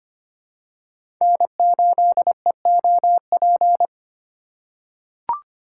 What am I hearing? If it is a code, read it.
Morse code, heard as a keyed tone, about 25 wpm, 700 Hz. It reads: N8EOP
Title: USA Call Signs - 5 Characters - 3x Character Spacing - 25wpm